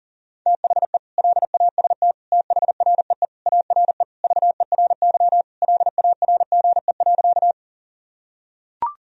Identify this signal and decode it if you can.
Morse code, heard as a keyed tone, about 40 wpm, 700 Hz. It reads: THE LAST THREE ARE VERY LARGE.
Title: Sentences from Top 1000 Words - Review Part 1 - 40wpm